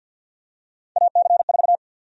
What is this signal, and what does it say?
Morse code, heard as a keyed tone, about 50 wpm, 700 Hz. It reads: AC4
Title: USA Call Sign Prefixes - 50wpm